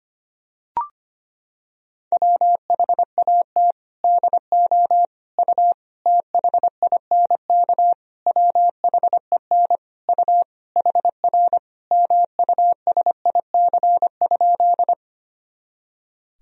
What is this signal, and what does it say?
Morse code, heard as a keyed tone, about 25 wpm, 700 Hz. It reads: WHAT DO U THINK WHEN U HR MUSIC?
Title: Sentences from Top 300 Words and Abbreviations 25wpm